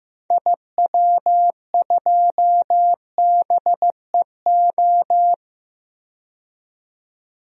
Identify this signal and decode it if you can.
Morse code, heard as a keyed tone, about 15 wpm, 700 Hz. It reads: IW2BEO
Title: DXCC-100 Call Signs - 6 Characters - 15wpm